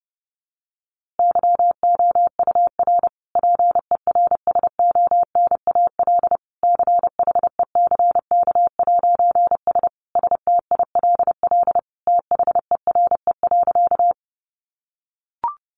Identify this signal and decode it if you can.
Morse code, heard as a keyed tone, about 30 wpm, 700 Hz. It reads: YOUR PERSONAL CHECK'S STILL THERE.
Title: A Sound Of Thunder 30wpm